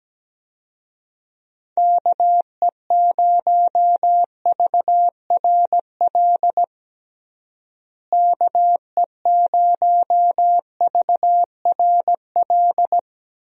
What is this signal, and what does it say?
Morse code, heard as a keyed tone, about 17 wpm, 700 Hz. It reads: KE0VRL KE0VRL
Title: USA Call Signs - 6 Characters - Repeated 3x - 17wpm